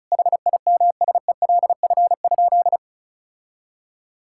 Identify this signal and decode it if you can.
Morse code, heard as a keyed tone, about 34 wpm, 700 Hz. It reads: HIMSELF?
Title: Sentences from Top 1600 Words 35wpm